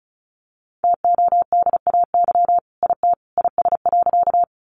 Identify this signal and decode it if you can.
Morse code, heard as a keyed tone, about 34 wpm, 700 Hz. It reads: TODAY IT IS.
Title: Sentences from Top 600 Words 35wpm